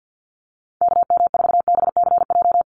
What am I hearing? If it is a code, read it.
Morse code, heard as a keyed tone, about 50 wpm, 700 Hz. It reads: KM4BCJ
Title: USA Call Signs 50wpm